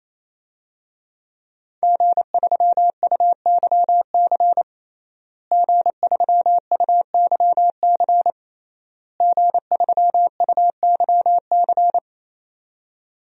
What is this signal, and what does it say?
Morse code, heard as a keyed tone, about 27 wpm, 700 Hz. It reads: G3UYC G3UYC G3UYC
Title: DXCC-100 Call Signs - 5 Characters - Repeated 3x - 28wpm